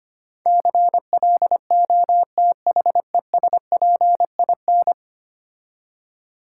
Morse code, heard as a keyed tone, about 28 wpm, 700 Hz. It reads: CLOTHESPIN
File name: Easy Words - Speed-Racing with Hints - 25wpm